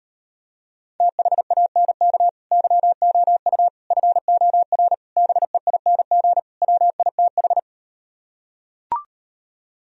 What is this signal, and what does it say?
Morse code, heard as a keyed tone, about 38 wpm, 700 Hz. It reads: THANK YOU FOR BEING WITH